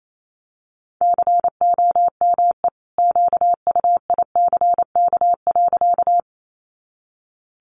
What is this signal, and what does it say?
Morse code, heard as a keyed tone, about 28 wpm, 700 Hz. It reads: COME QUICK.
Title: Sentences from Top 500 Words 28wpm